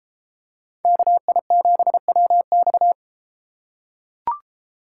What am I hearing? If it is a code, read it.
Morse code, heard as a keyed tone, about 33 wpm, 700 Hz. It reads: KI7WX